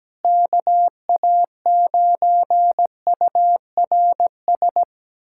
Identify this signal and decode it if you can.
Morse code, heard as a keyed tone, about 17 wpm, 700 Hz. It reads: KA9URS
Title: USA Callsign DE Callsign 17wpm